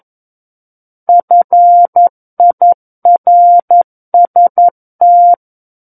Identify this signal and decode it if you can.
Morse code, heard as a keyed tone, about 11 wpm, 700 Hz. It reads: FIRST
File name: 11 WPM